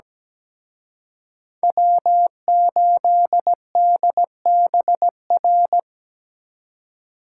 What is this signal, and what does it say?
Morse code, heard as a keyed tone, about 20 wpm, 700 Hz. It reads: W8DBR
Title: Speed Racing - USA 1x3 Call Signs 17wpm